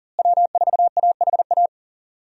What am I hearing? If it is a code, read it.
Morse code, heard as a keyed tone, about 40 wpm, 700 Hz. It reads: W4AHA
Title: USA Call Signs - 5 Characters - Repeated 3x - 40wpm